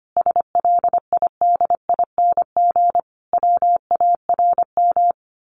Morse code, heard as a keyed tone, about 25 wpm, 700 Hz. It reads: SLIDING WARM
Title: A Sound Of Thunder 25wpm